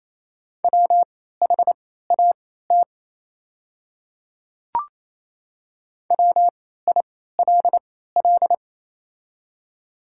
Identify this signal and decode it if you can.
Morse code, heard as a keyed tone, about 28 wpm, spaced as for 16 wpm, 700 Hz. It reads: WHAT WILL
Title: 3x Character Spacing - Top 100 Words - 28wpm